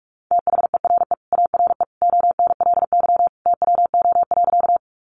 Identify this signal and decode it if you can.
Morse code, heard as a keyed tone, about 44 wpm, 700 Hz. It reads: THERE ARE ONLY TWO.
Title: Sentences from Top 200 Words 45wpm